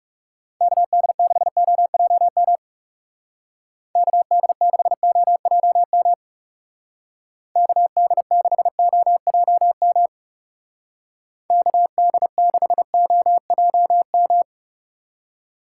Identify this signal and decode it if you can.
Morse code, heard as a keyed tone, about 37 wpm, 700 Hz. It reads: KD6OJM KD6OJM KD6OJM KD6OJM
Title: Speed Racing - USA Call Signs 30wpm